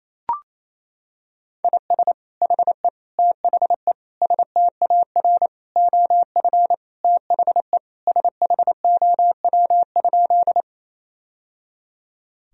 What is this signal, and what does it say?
Morse code, heard as a keyed tone, about 28 wpm, 700 Hz. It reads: IS HE THE STAR OF THE SHOW?
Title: Sentences from Top 1000 Words - Review Part 2 - 28wpm